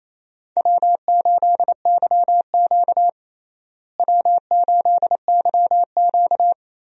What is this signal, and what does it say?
Morse code, heard as a keyed tone, about 28 wpm, 700 Hz. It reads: W8YQ W8YQ
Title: USA Call Signs - 4 Characters - Repeated 3x - 28wpm